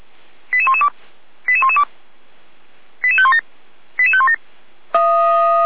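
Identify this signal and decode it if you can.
Five-tone selective call (ZVEI), heard as quick runs of five tones, tones of about 70 ms. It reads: 88112 88112 88517 88517
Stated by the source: ZVEI five-tone call 88191 received by SDR.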